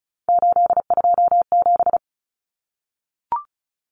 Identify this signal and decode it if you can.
Morse code, heard as a keyed tone, about 34 wpm, 700 Hz. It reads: 827
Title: Common Cut Numbers - 3-Digit 35wpm